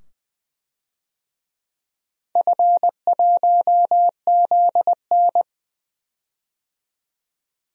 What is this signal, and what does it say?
Morse code, heard as a keyed tone, about 21 wpm, 700 Hz. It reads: F1ZN
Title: DXCC-100 Call Signs - 4 Characters - 20wpm